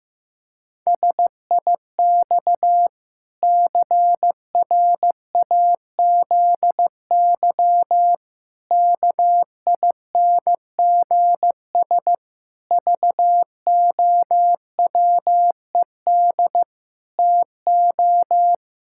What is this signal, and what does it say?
Morse code, heard as a keyed tone, about 15 wpm, 700 Hz. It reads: SIX CRAZY KINGS VOWED TO
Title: Pangram Challenge v2 - 15wpm